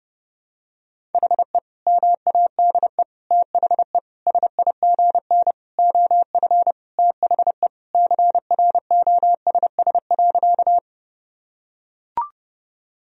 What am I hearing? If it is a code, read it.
Morse code, heard as a keyed tone, about 30 wpm, 700 Hz. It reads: HE MADE THE SIGN OF THE CROSS.
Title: Sentences from Top 1000 Words - Review Part 1 - 30wpm